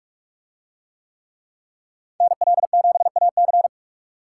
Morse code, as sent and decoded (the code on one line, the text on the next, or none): -. .-.. --... .- -.-.
NL7AC